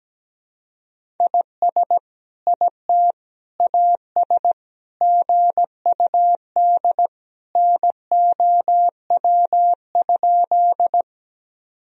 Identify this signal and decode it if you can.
Morse code, heard as a keyed tone, about 17 wpm, 700 Hz. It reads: IS IT AS GUD NOW?
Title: Sentences from Top 200 Words and Abbreviations 17wpm